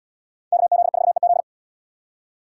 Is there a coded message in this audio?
Yes: Morse code from 0.5 to 1.4 s, heard as a keyed tone, about 77 wpm, 700 Hz. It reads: KZ4EZ